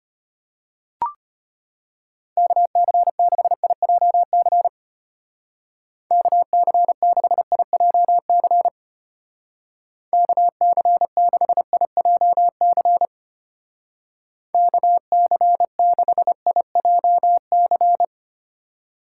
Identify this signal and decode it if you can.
Morse code, heard as a keyed tone, about 32 wpm, 700 Hz. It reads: KC6IJC KC6IJC KC6IJC KC6IJC